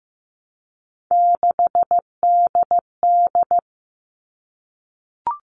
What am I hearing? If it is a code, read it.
Morse code, heard as a keyed tone, about 15 wpm, 700 Hz. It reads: 6DD
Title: All Cut Numbers - 3-Digit 15wpm